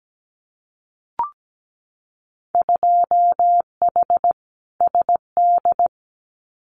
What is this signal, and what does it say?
Morse code, heard as a keyed tone, about 17 wpm, 700 Hz. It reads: 2H SD
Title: Winter Field Day Exchanges 17wpm